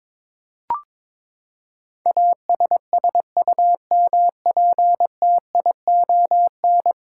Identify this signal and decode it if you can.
Morse code, heard as a keyed tone, about 22 wpm, 700 Hz. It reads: ASSUMPTION
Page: morse-code-ninja-practice-22wpm